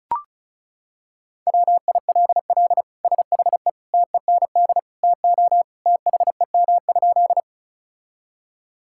Morse code, heard as a keyed tone, about 36 wpm, 700 Hz. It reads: WILL SHE TEND TO THEM?
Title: Sentences from Top 1000 Words - Review Part 5 - 35wpm